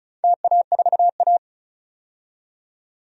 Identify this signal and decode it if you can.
Morse code, heard as a keyed tone, about 36 wpm, 700 Hz. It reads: TA4A
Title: DXCC-100 Call Signs - 4 Characters - 35wpm